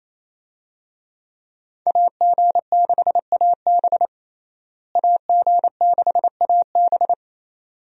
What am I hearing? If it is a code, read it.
Morse code, heard as a keyed tone, about 28 wpm, 700 Hz. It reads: AG6AB AG6AB
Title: USA Call Signs - 5 Characters - Repeated 2x - 28wpm